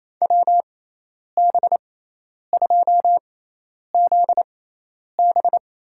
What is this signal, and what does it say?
Morse code, heard as a keyed tone, about 27 wpm, spaced as for 10 wpm, 700 Hz. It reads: WB2ZB
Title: USA Call Signs - 6x Character Spacing - 28wpm